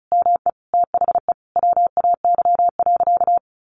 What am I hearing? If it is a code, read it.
Morse code, heard as a keyed tone, about 34 wpm, 700 Hz. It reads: ME THE WAY.
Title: Sentences from Top 1500 Words 35wpm